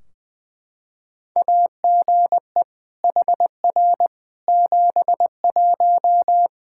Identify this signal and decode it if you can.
Morse code, heard as a keyed tone, about 20 wpm, 700 Hz. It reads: AGE HR 71
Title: Practice QSO - Exchange 3 - More Info - 20wpm